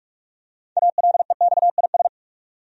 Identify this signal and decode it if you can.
Morse code, heard as a keyed tone, about 46 wpm, 700 Hz. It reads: ALEXIS
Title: USA Common Names - 45wpm